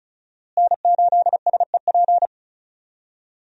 Morse code, heard as a keyed tone, about 34 wpm, 700 Hz. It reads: N8SEP
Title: USA Call Signs - 5 Characters - Repeated 3x - 35wpm